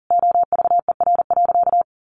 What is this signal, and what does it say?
Morse code, heard as a keyed tone, about 40 wpm, 700 Hz. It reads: OVER.
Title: Sentences from Top 1200 Words 40wpm